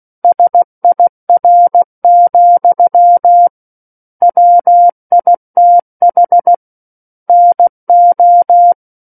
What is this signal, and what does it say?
Morse code, heard as a keyed tone, about 16 wpm, 700 Hz. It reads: SIR, WITH NO
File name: 03-Chapter-03 16 WPM